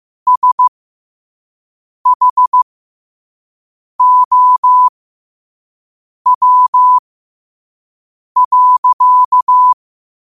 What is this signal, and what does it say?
Morse code, heard as a keyed tone, about 15 wpm, spaced as for 5 wpm, 1000 Hz. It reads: SHOW.